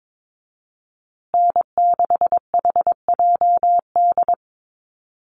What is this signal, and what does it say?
Morse code, heard as a keyed tone, about 22 wpm, 700 Hz. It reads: N6HJD
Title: USA Call Signs Repeated 3x 22wpm